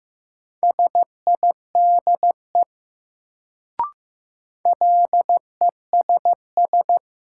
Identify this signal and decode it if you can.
Morse code, heard as a keyed tone, about 15 wpm, 700 Hz. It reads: SIDE LESS
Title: Sentences from Top 400 Words - New Word Drill - 15wpm